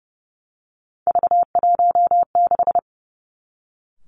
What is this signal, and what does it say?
Morse code, heard as a keyed tone, about 30 wpm, 700 Hz. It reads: V16
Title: Three Challenging Letter-Number Combinations 30wpm